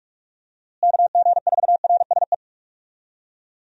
Morse code, heard as a keyed tone, about 46 wpm, 700 Hz. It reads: KG4RSE
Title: USA Call Signs - 6 Characters - Repeated 2x - 45wpm